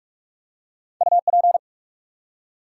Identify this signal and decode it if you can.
Morse code, heard as a keyed tone, about 44 wpm, 700 Hz. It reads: UP
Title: Two Letter Words - 45wpm